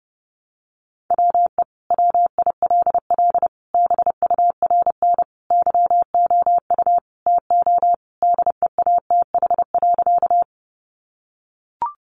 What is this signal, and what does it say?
Morse code, heard as a keyed tone, about 30 wpm, 700 Hz. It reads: WE WILL BURN YOU TO DEATH.
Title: Sentences from Top 1300 Words 30wpm